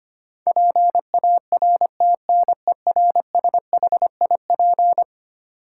Morse code, heard as a keyed tone, about 25 wpm, 700 Hz. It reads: PARTNERSHIP
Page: morse-code-ninja-practice-25wpm